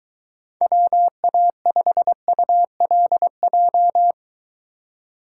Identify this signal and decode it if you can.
Morse code, heard as a keyed tone, about 23 wpm, 700 Hz. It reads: WA5ULJ